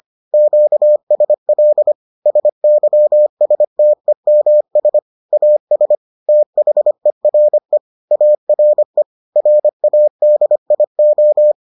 Morse code, heard as a keyed tone, about 25 wpm, 600 Hz. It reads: QSL SYSTEMS AS THERE ARE RADIO